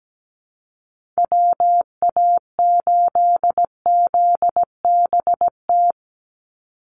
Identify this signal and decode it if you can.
Morse code, heard as a keyed tone, about 17 wpm, 700 Hz. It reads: WA8ZBT
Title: Call Signs Repeated 3x - 17wpm